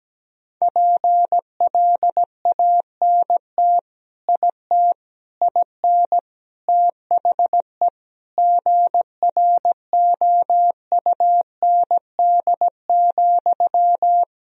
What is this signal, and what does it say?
Morse code, heard as a keyed tone, about 17 wpm, 700 Hz. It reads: PLANT IT IN THE GROUND,
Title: Sentences from Top 1000 Words - Review Part 2 - 17wpm